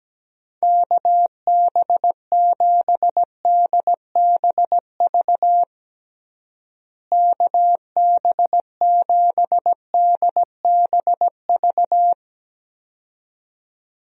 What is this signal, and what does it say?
Morse code, heard as a keyed tone, about 17 wpm, 700 Hz. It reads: KB7DBV KB7DBV
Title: USA Call Signs - 6 Characters - Repeated 2x - 17wpm